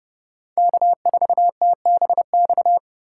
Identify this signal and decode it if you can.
Morse code, heard as a keyed tone, about 30 wpm, 700 Hz. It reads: K4TBX
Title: USA Call Signs - 4x Character Spacing - 30wpm